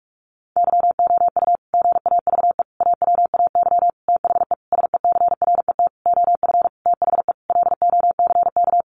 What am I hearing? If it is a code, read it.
Morse code, heard as a keyed tone, about 46 wpm, 700 Hz. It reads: YOU GAVE AWAY THE SECRET OF THE LOCK
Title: Sentences from Top 1300 Words 45wpm